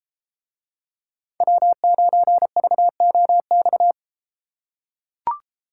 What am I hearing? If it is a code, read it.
Morse code, heard as a keyed tone, about 33 wpm, 700 Hz. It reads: W9VOX